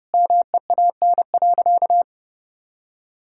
Morse code, heard as a keyed tone, about 30 wpm, 700 Hz. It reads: MEAN.
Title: Sentences from Top 1000 Words - Review Part 6 - 30wpm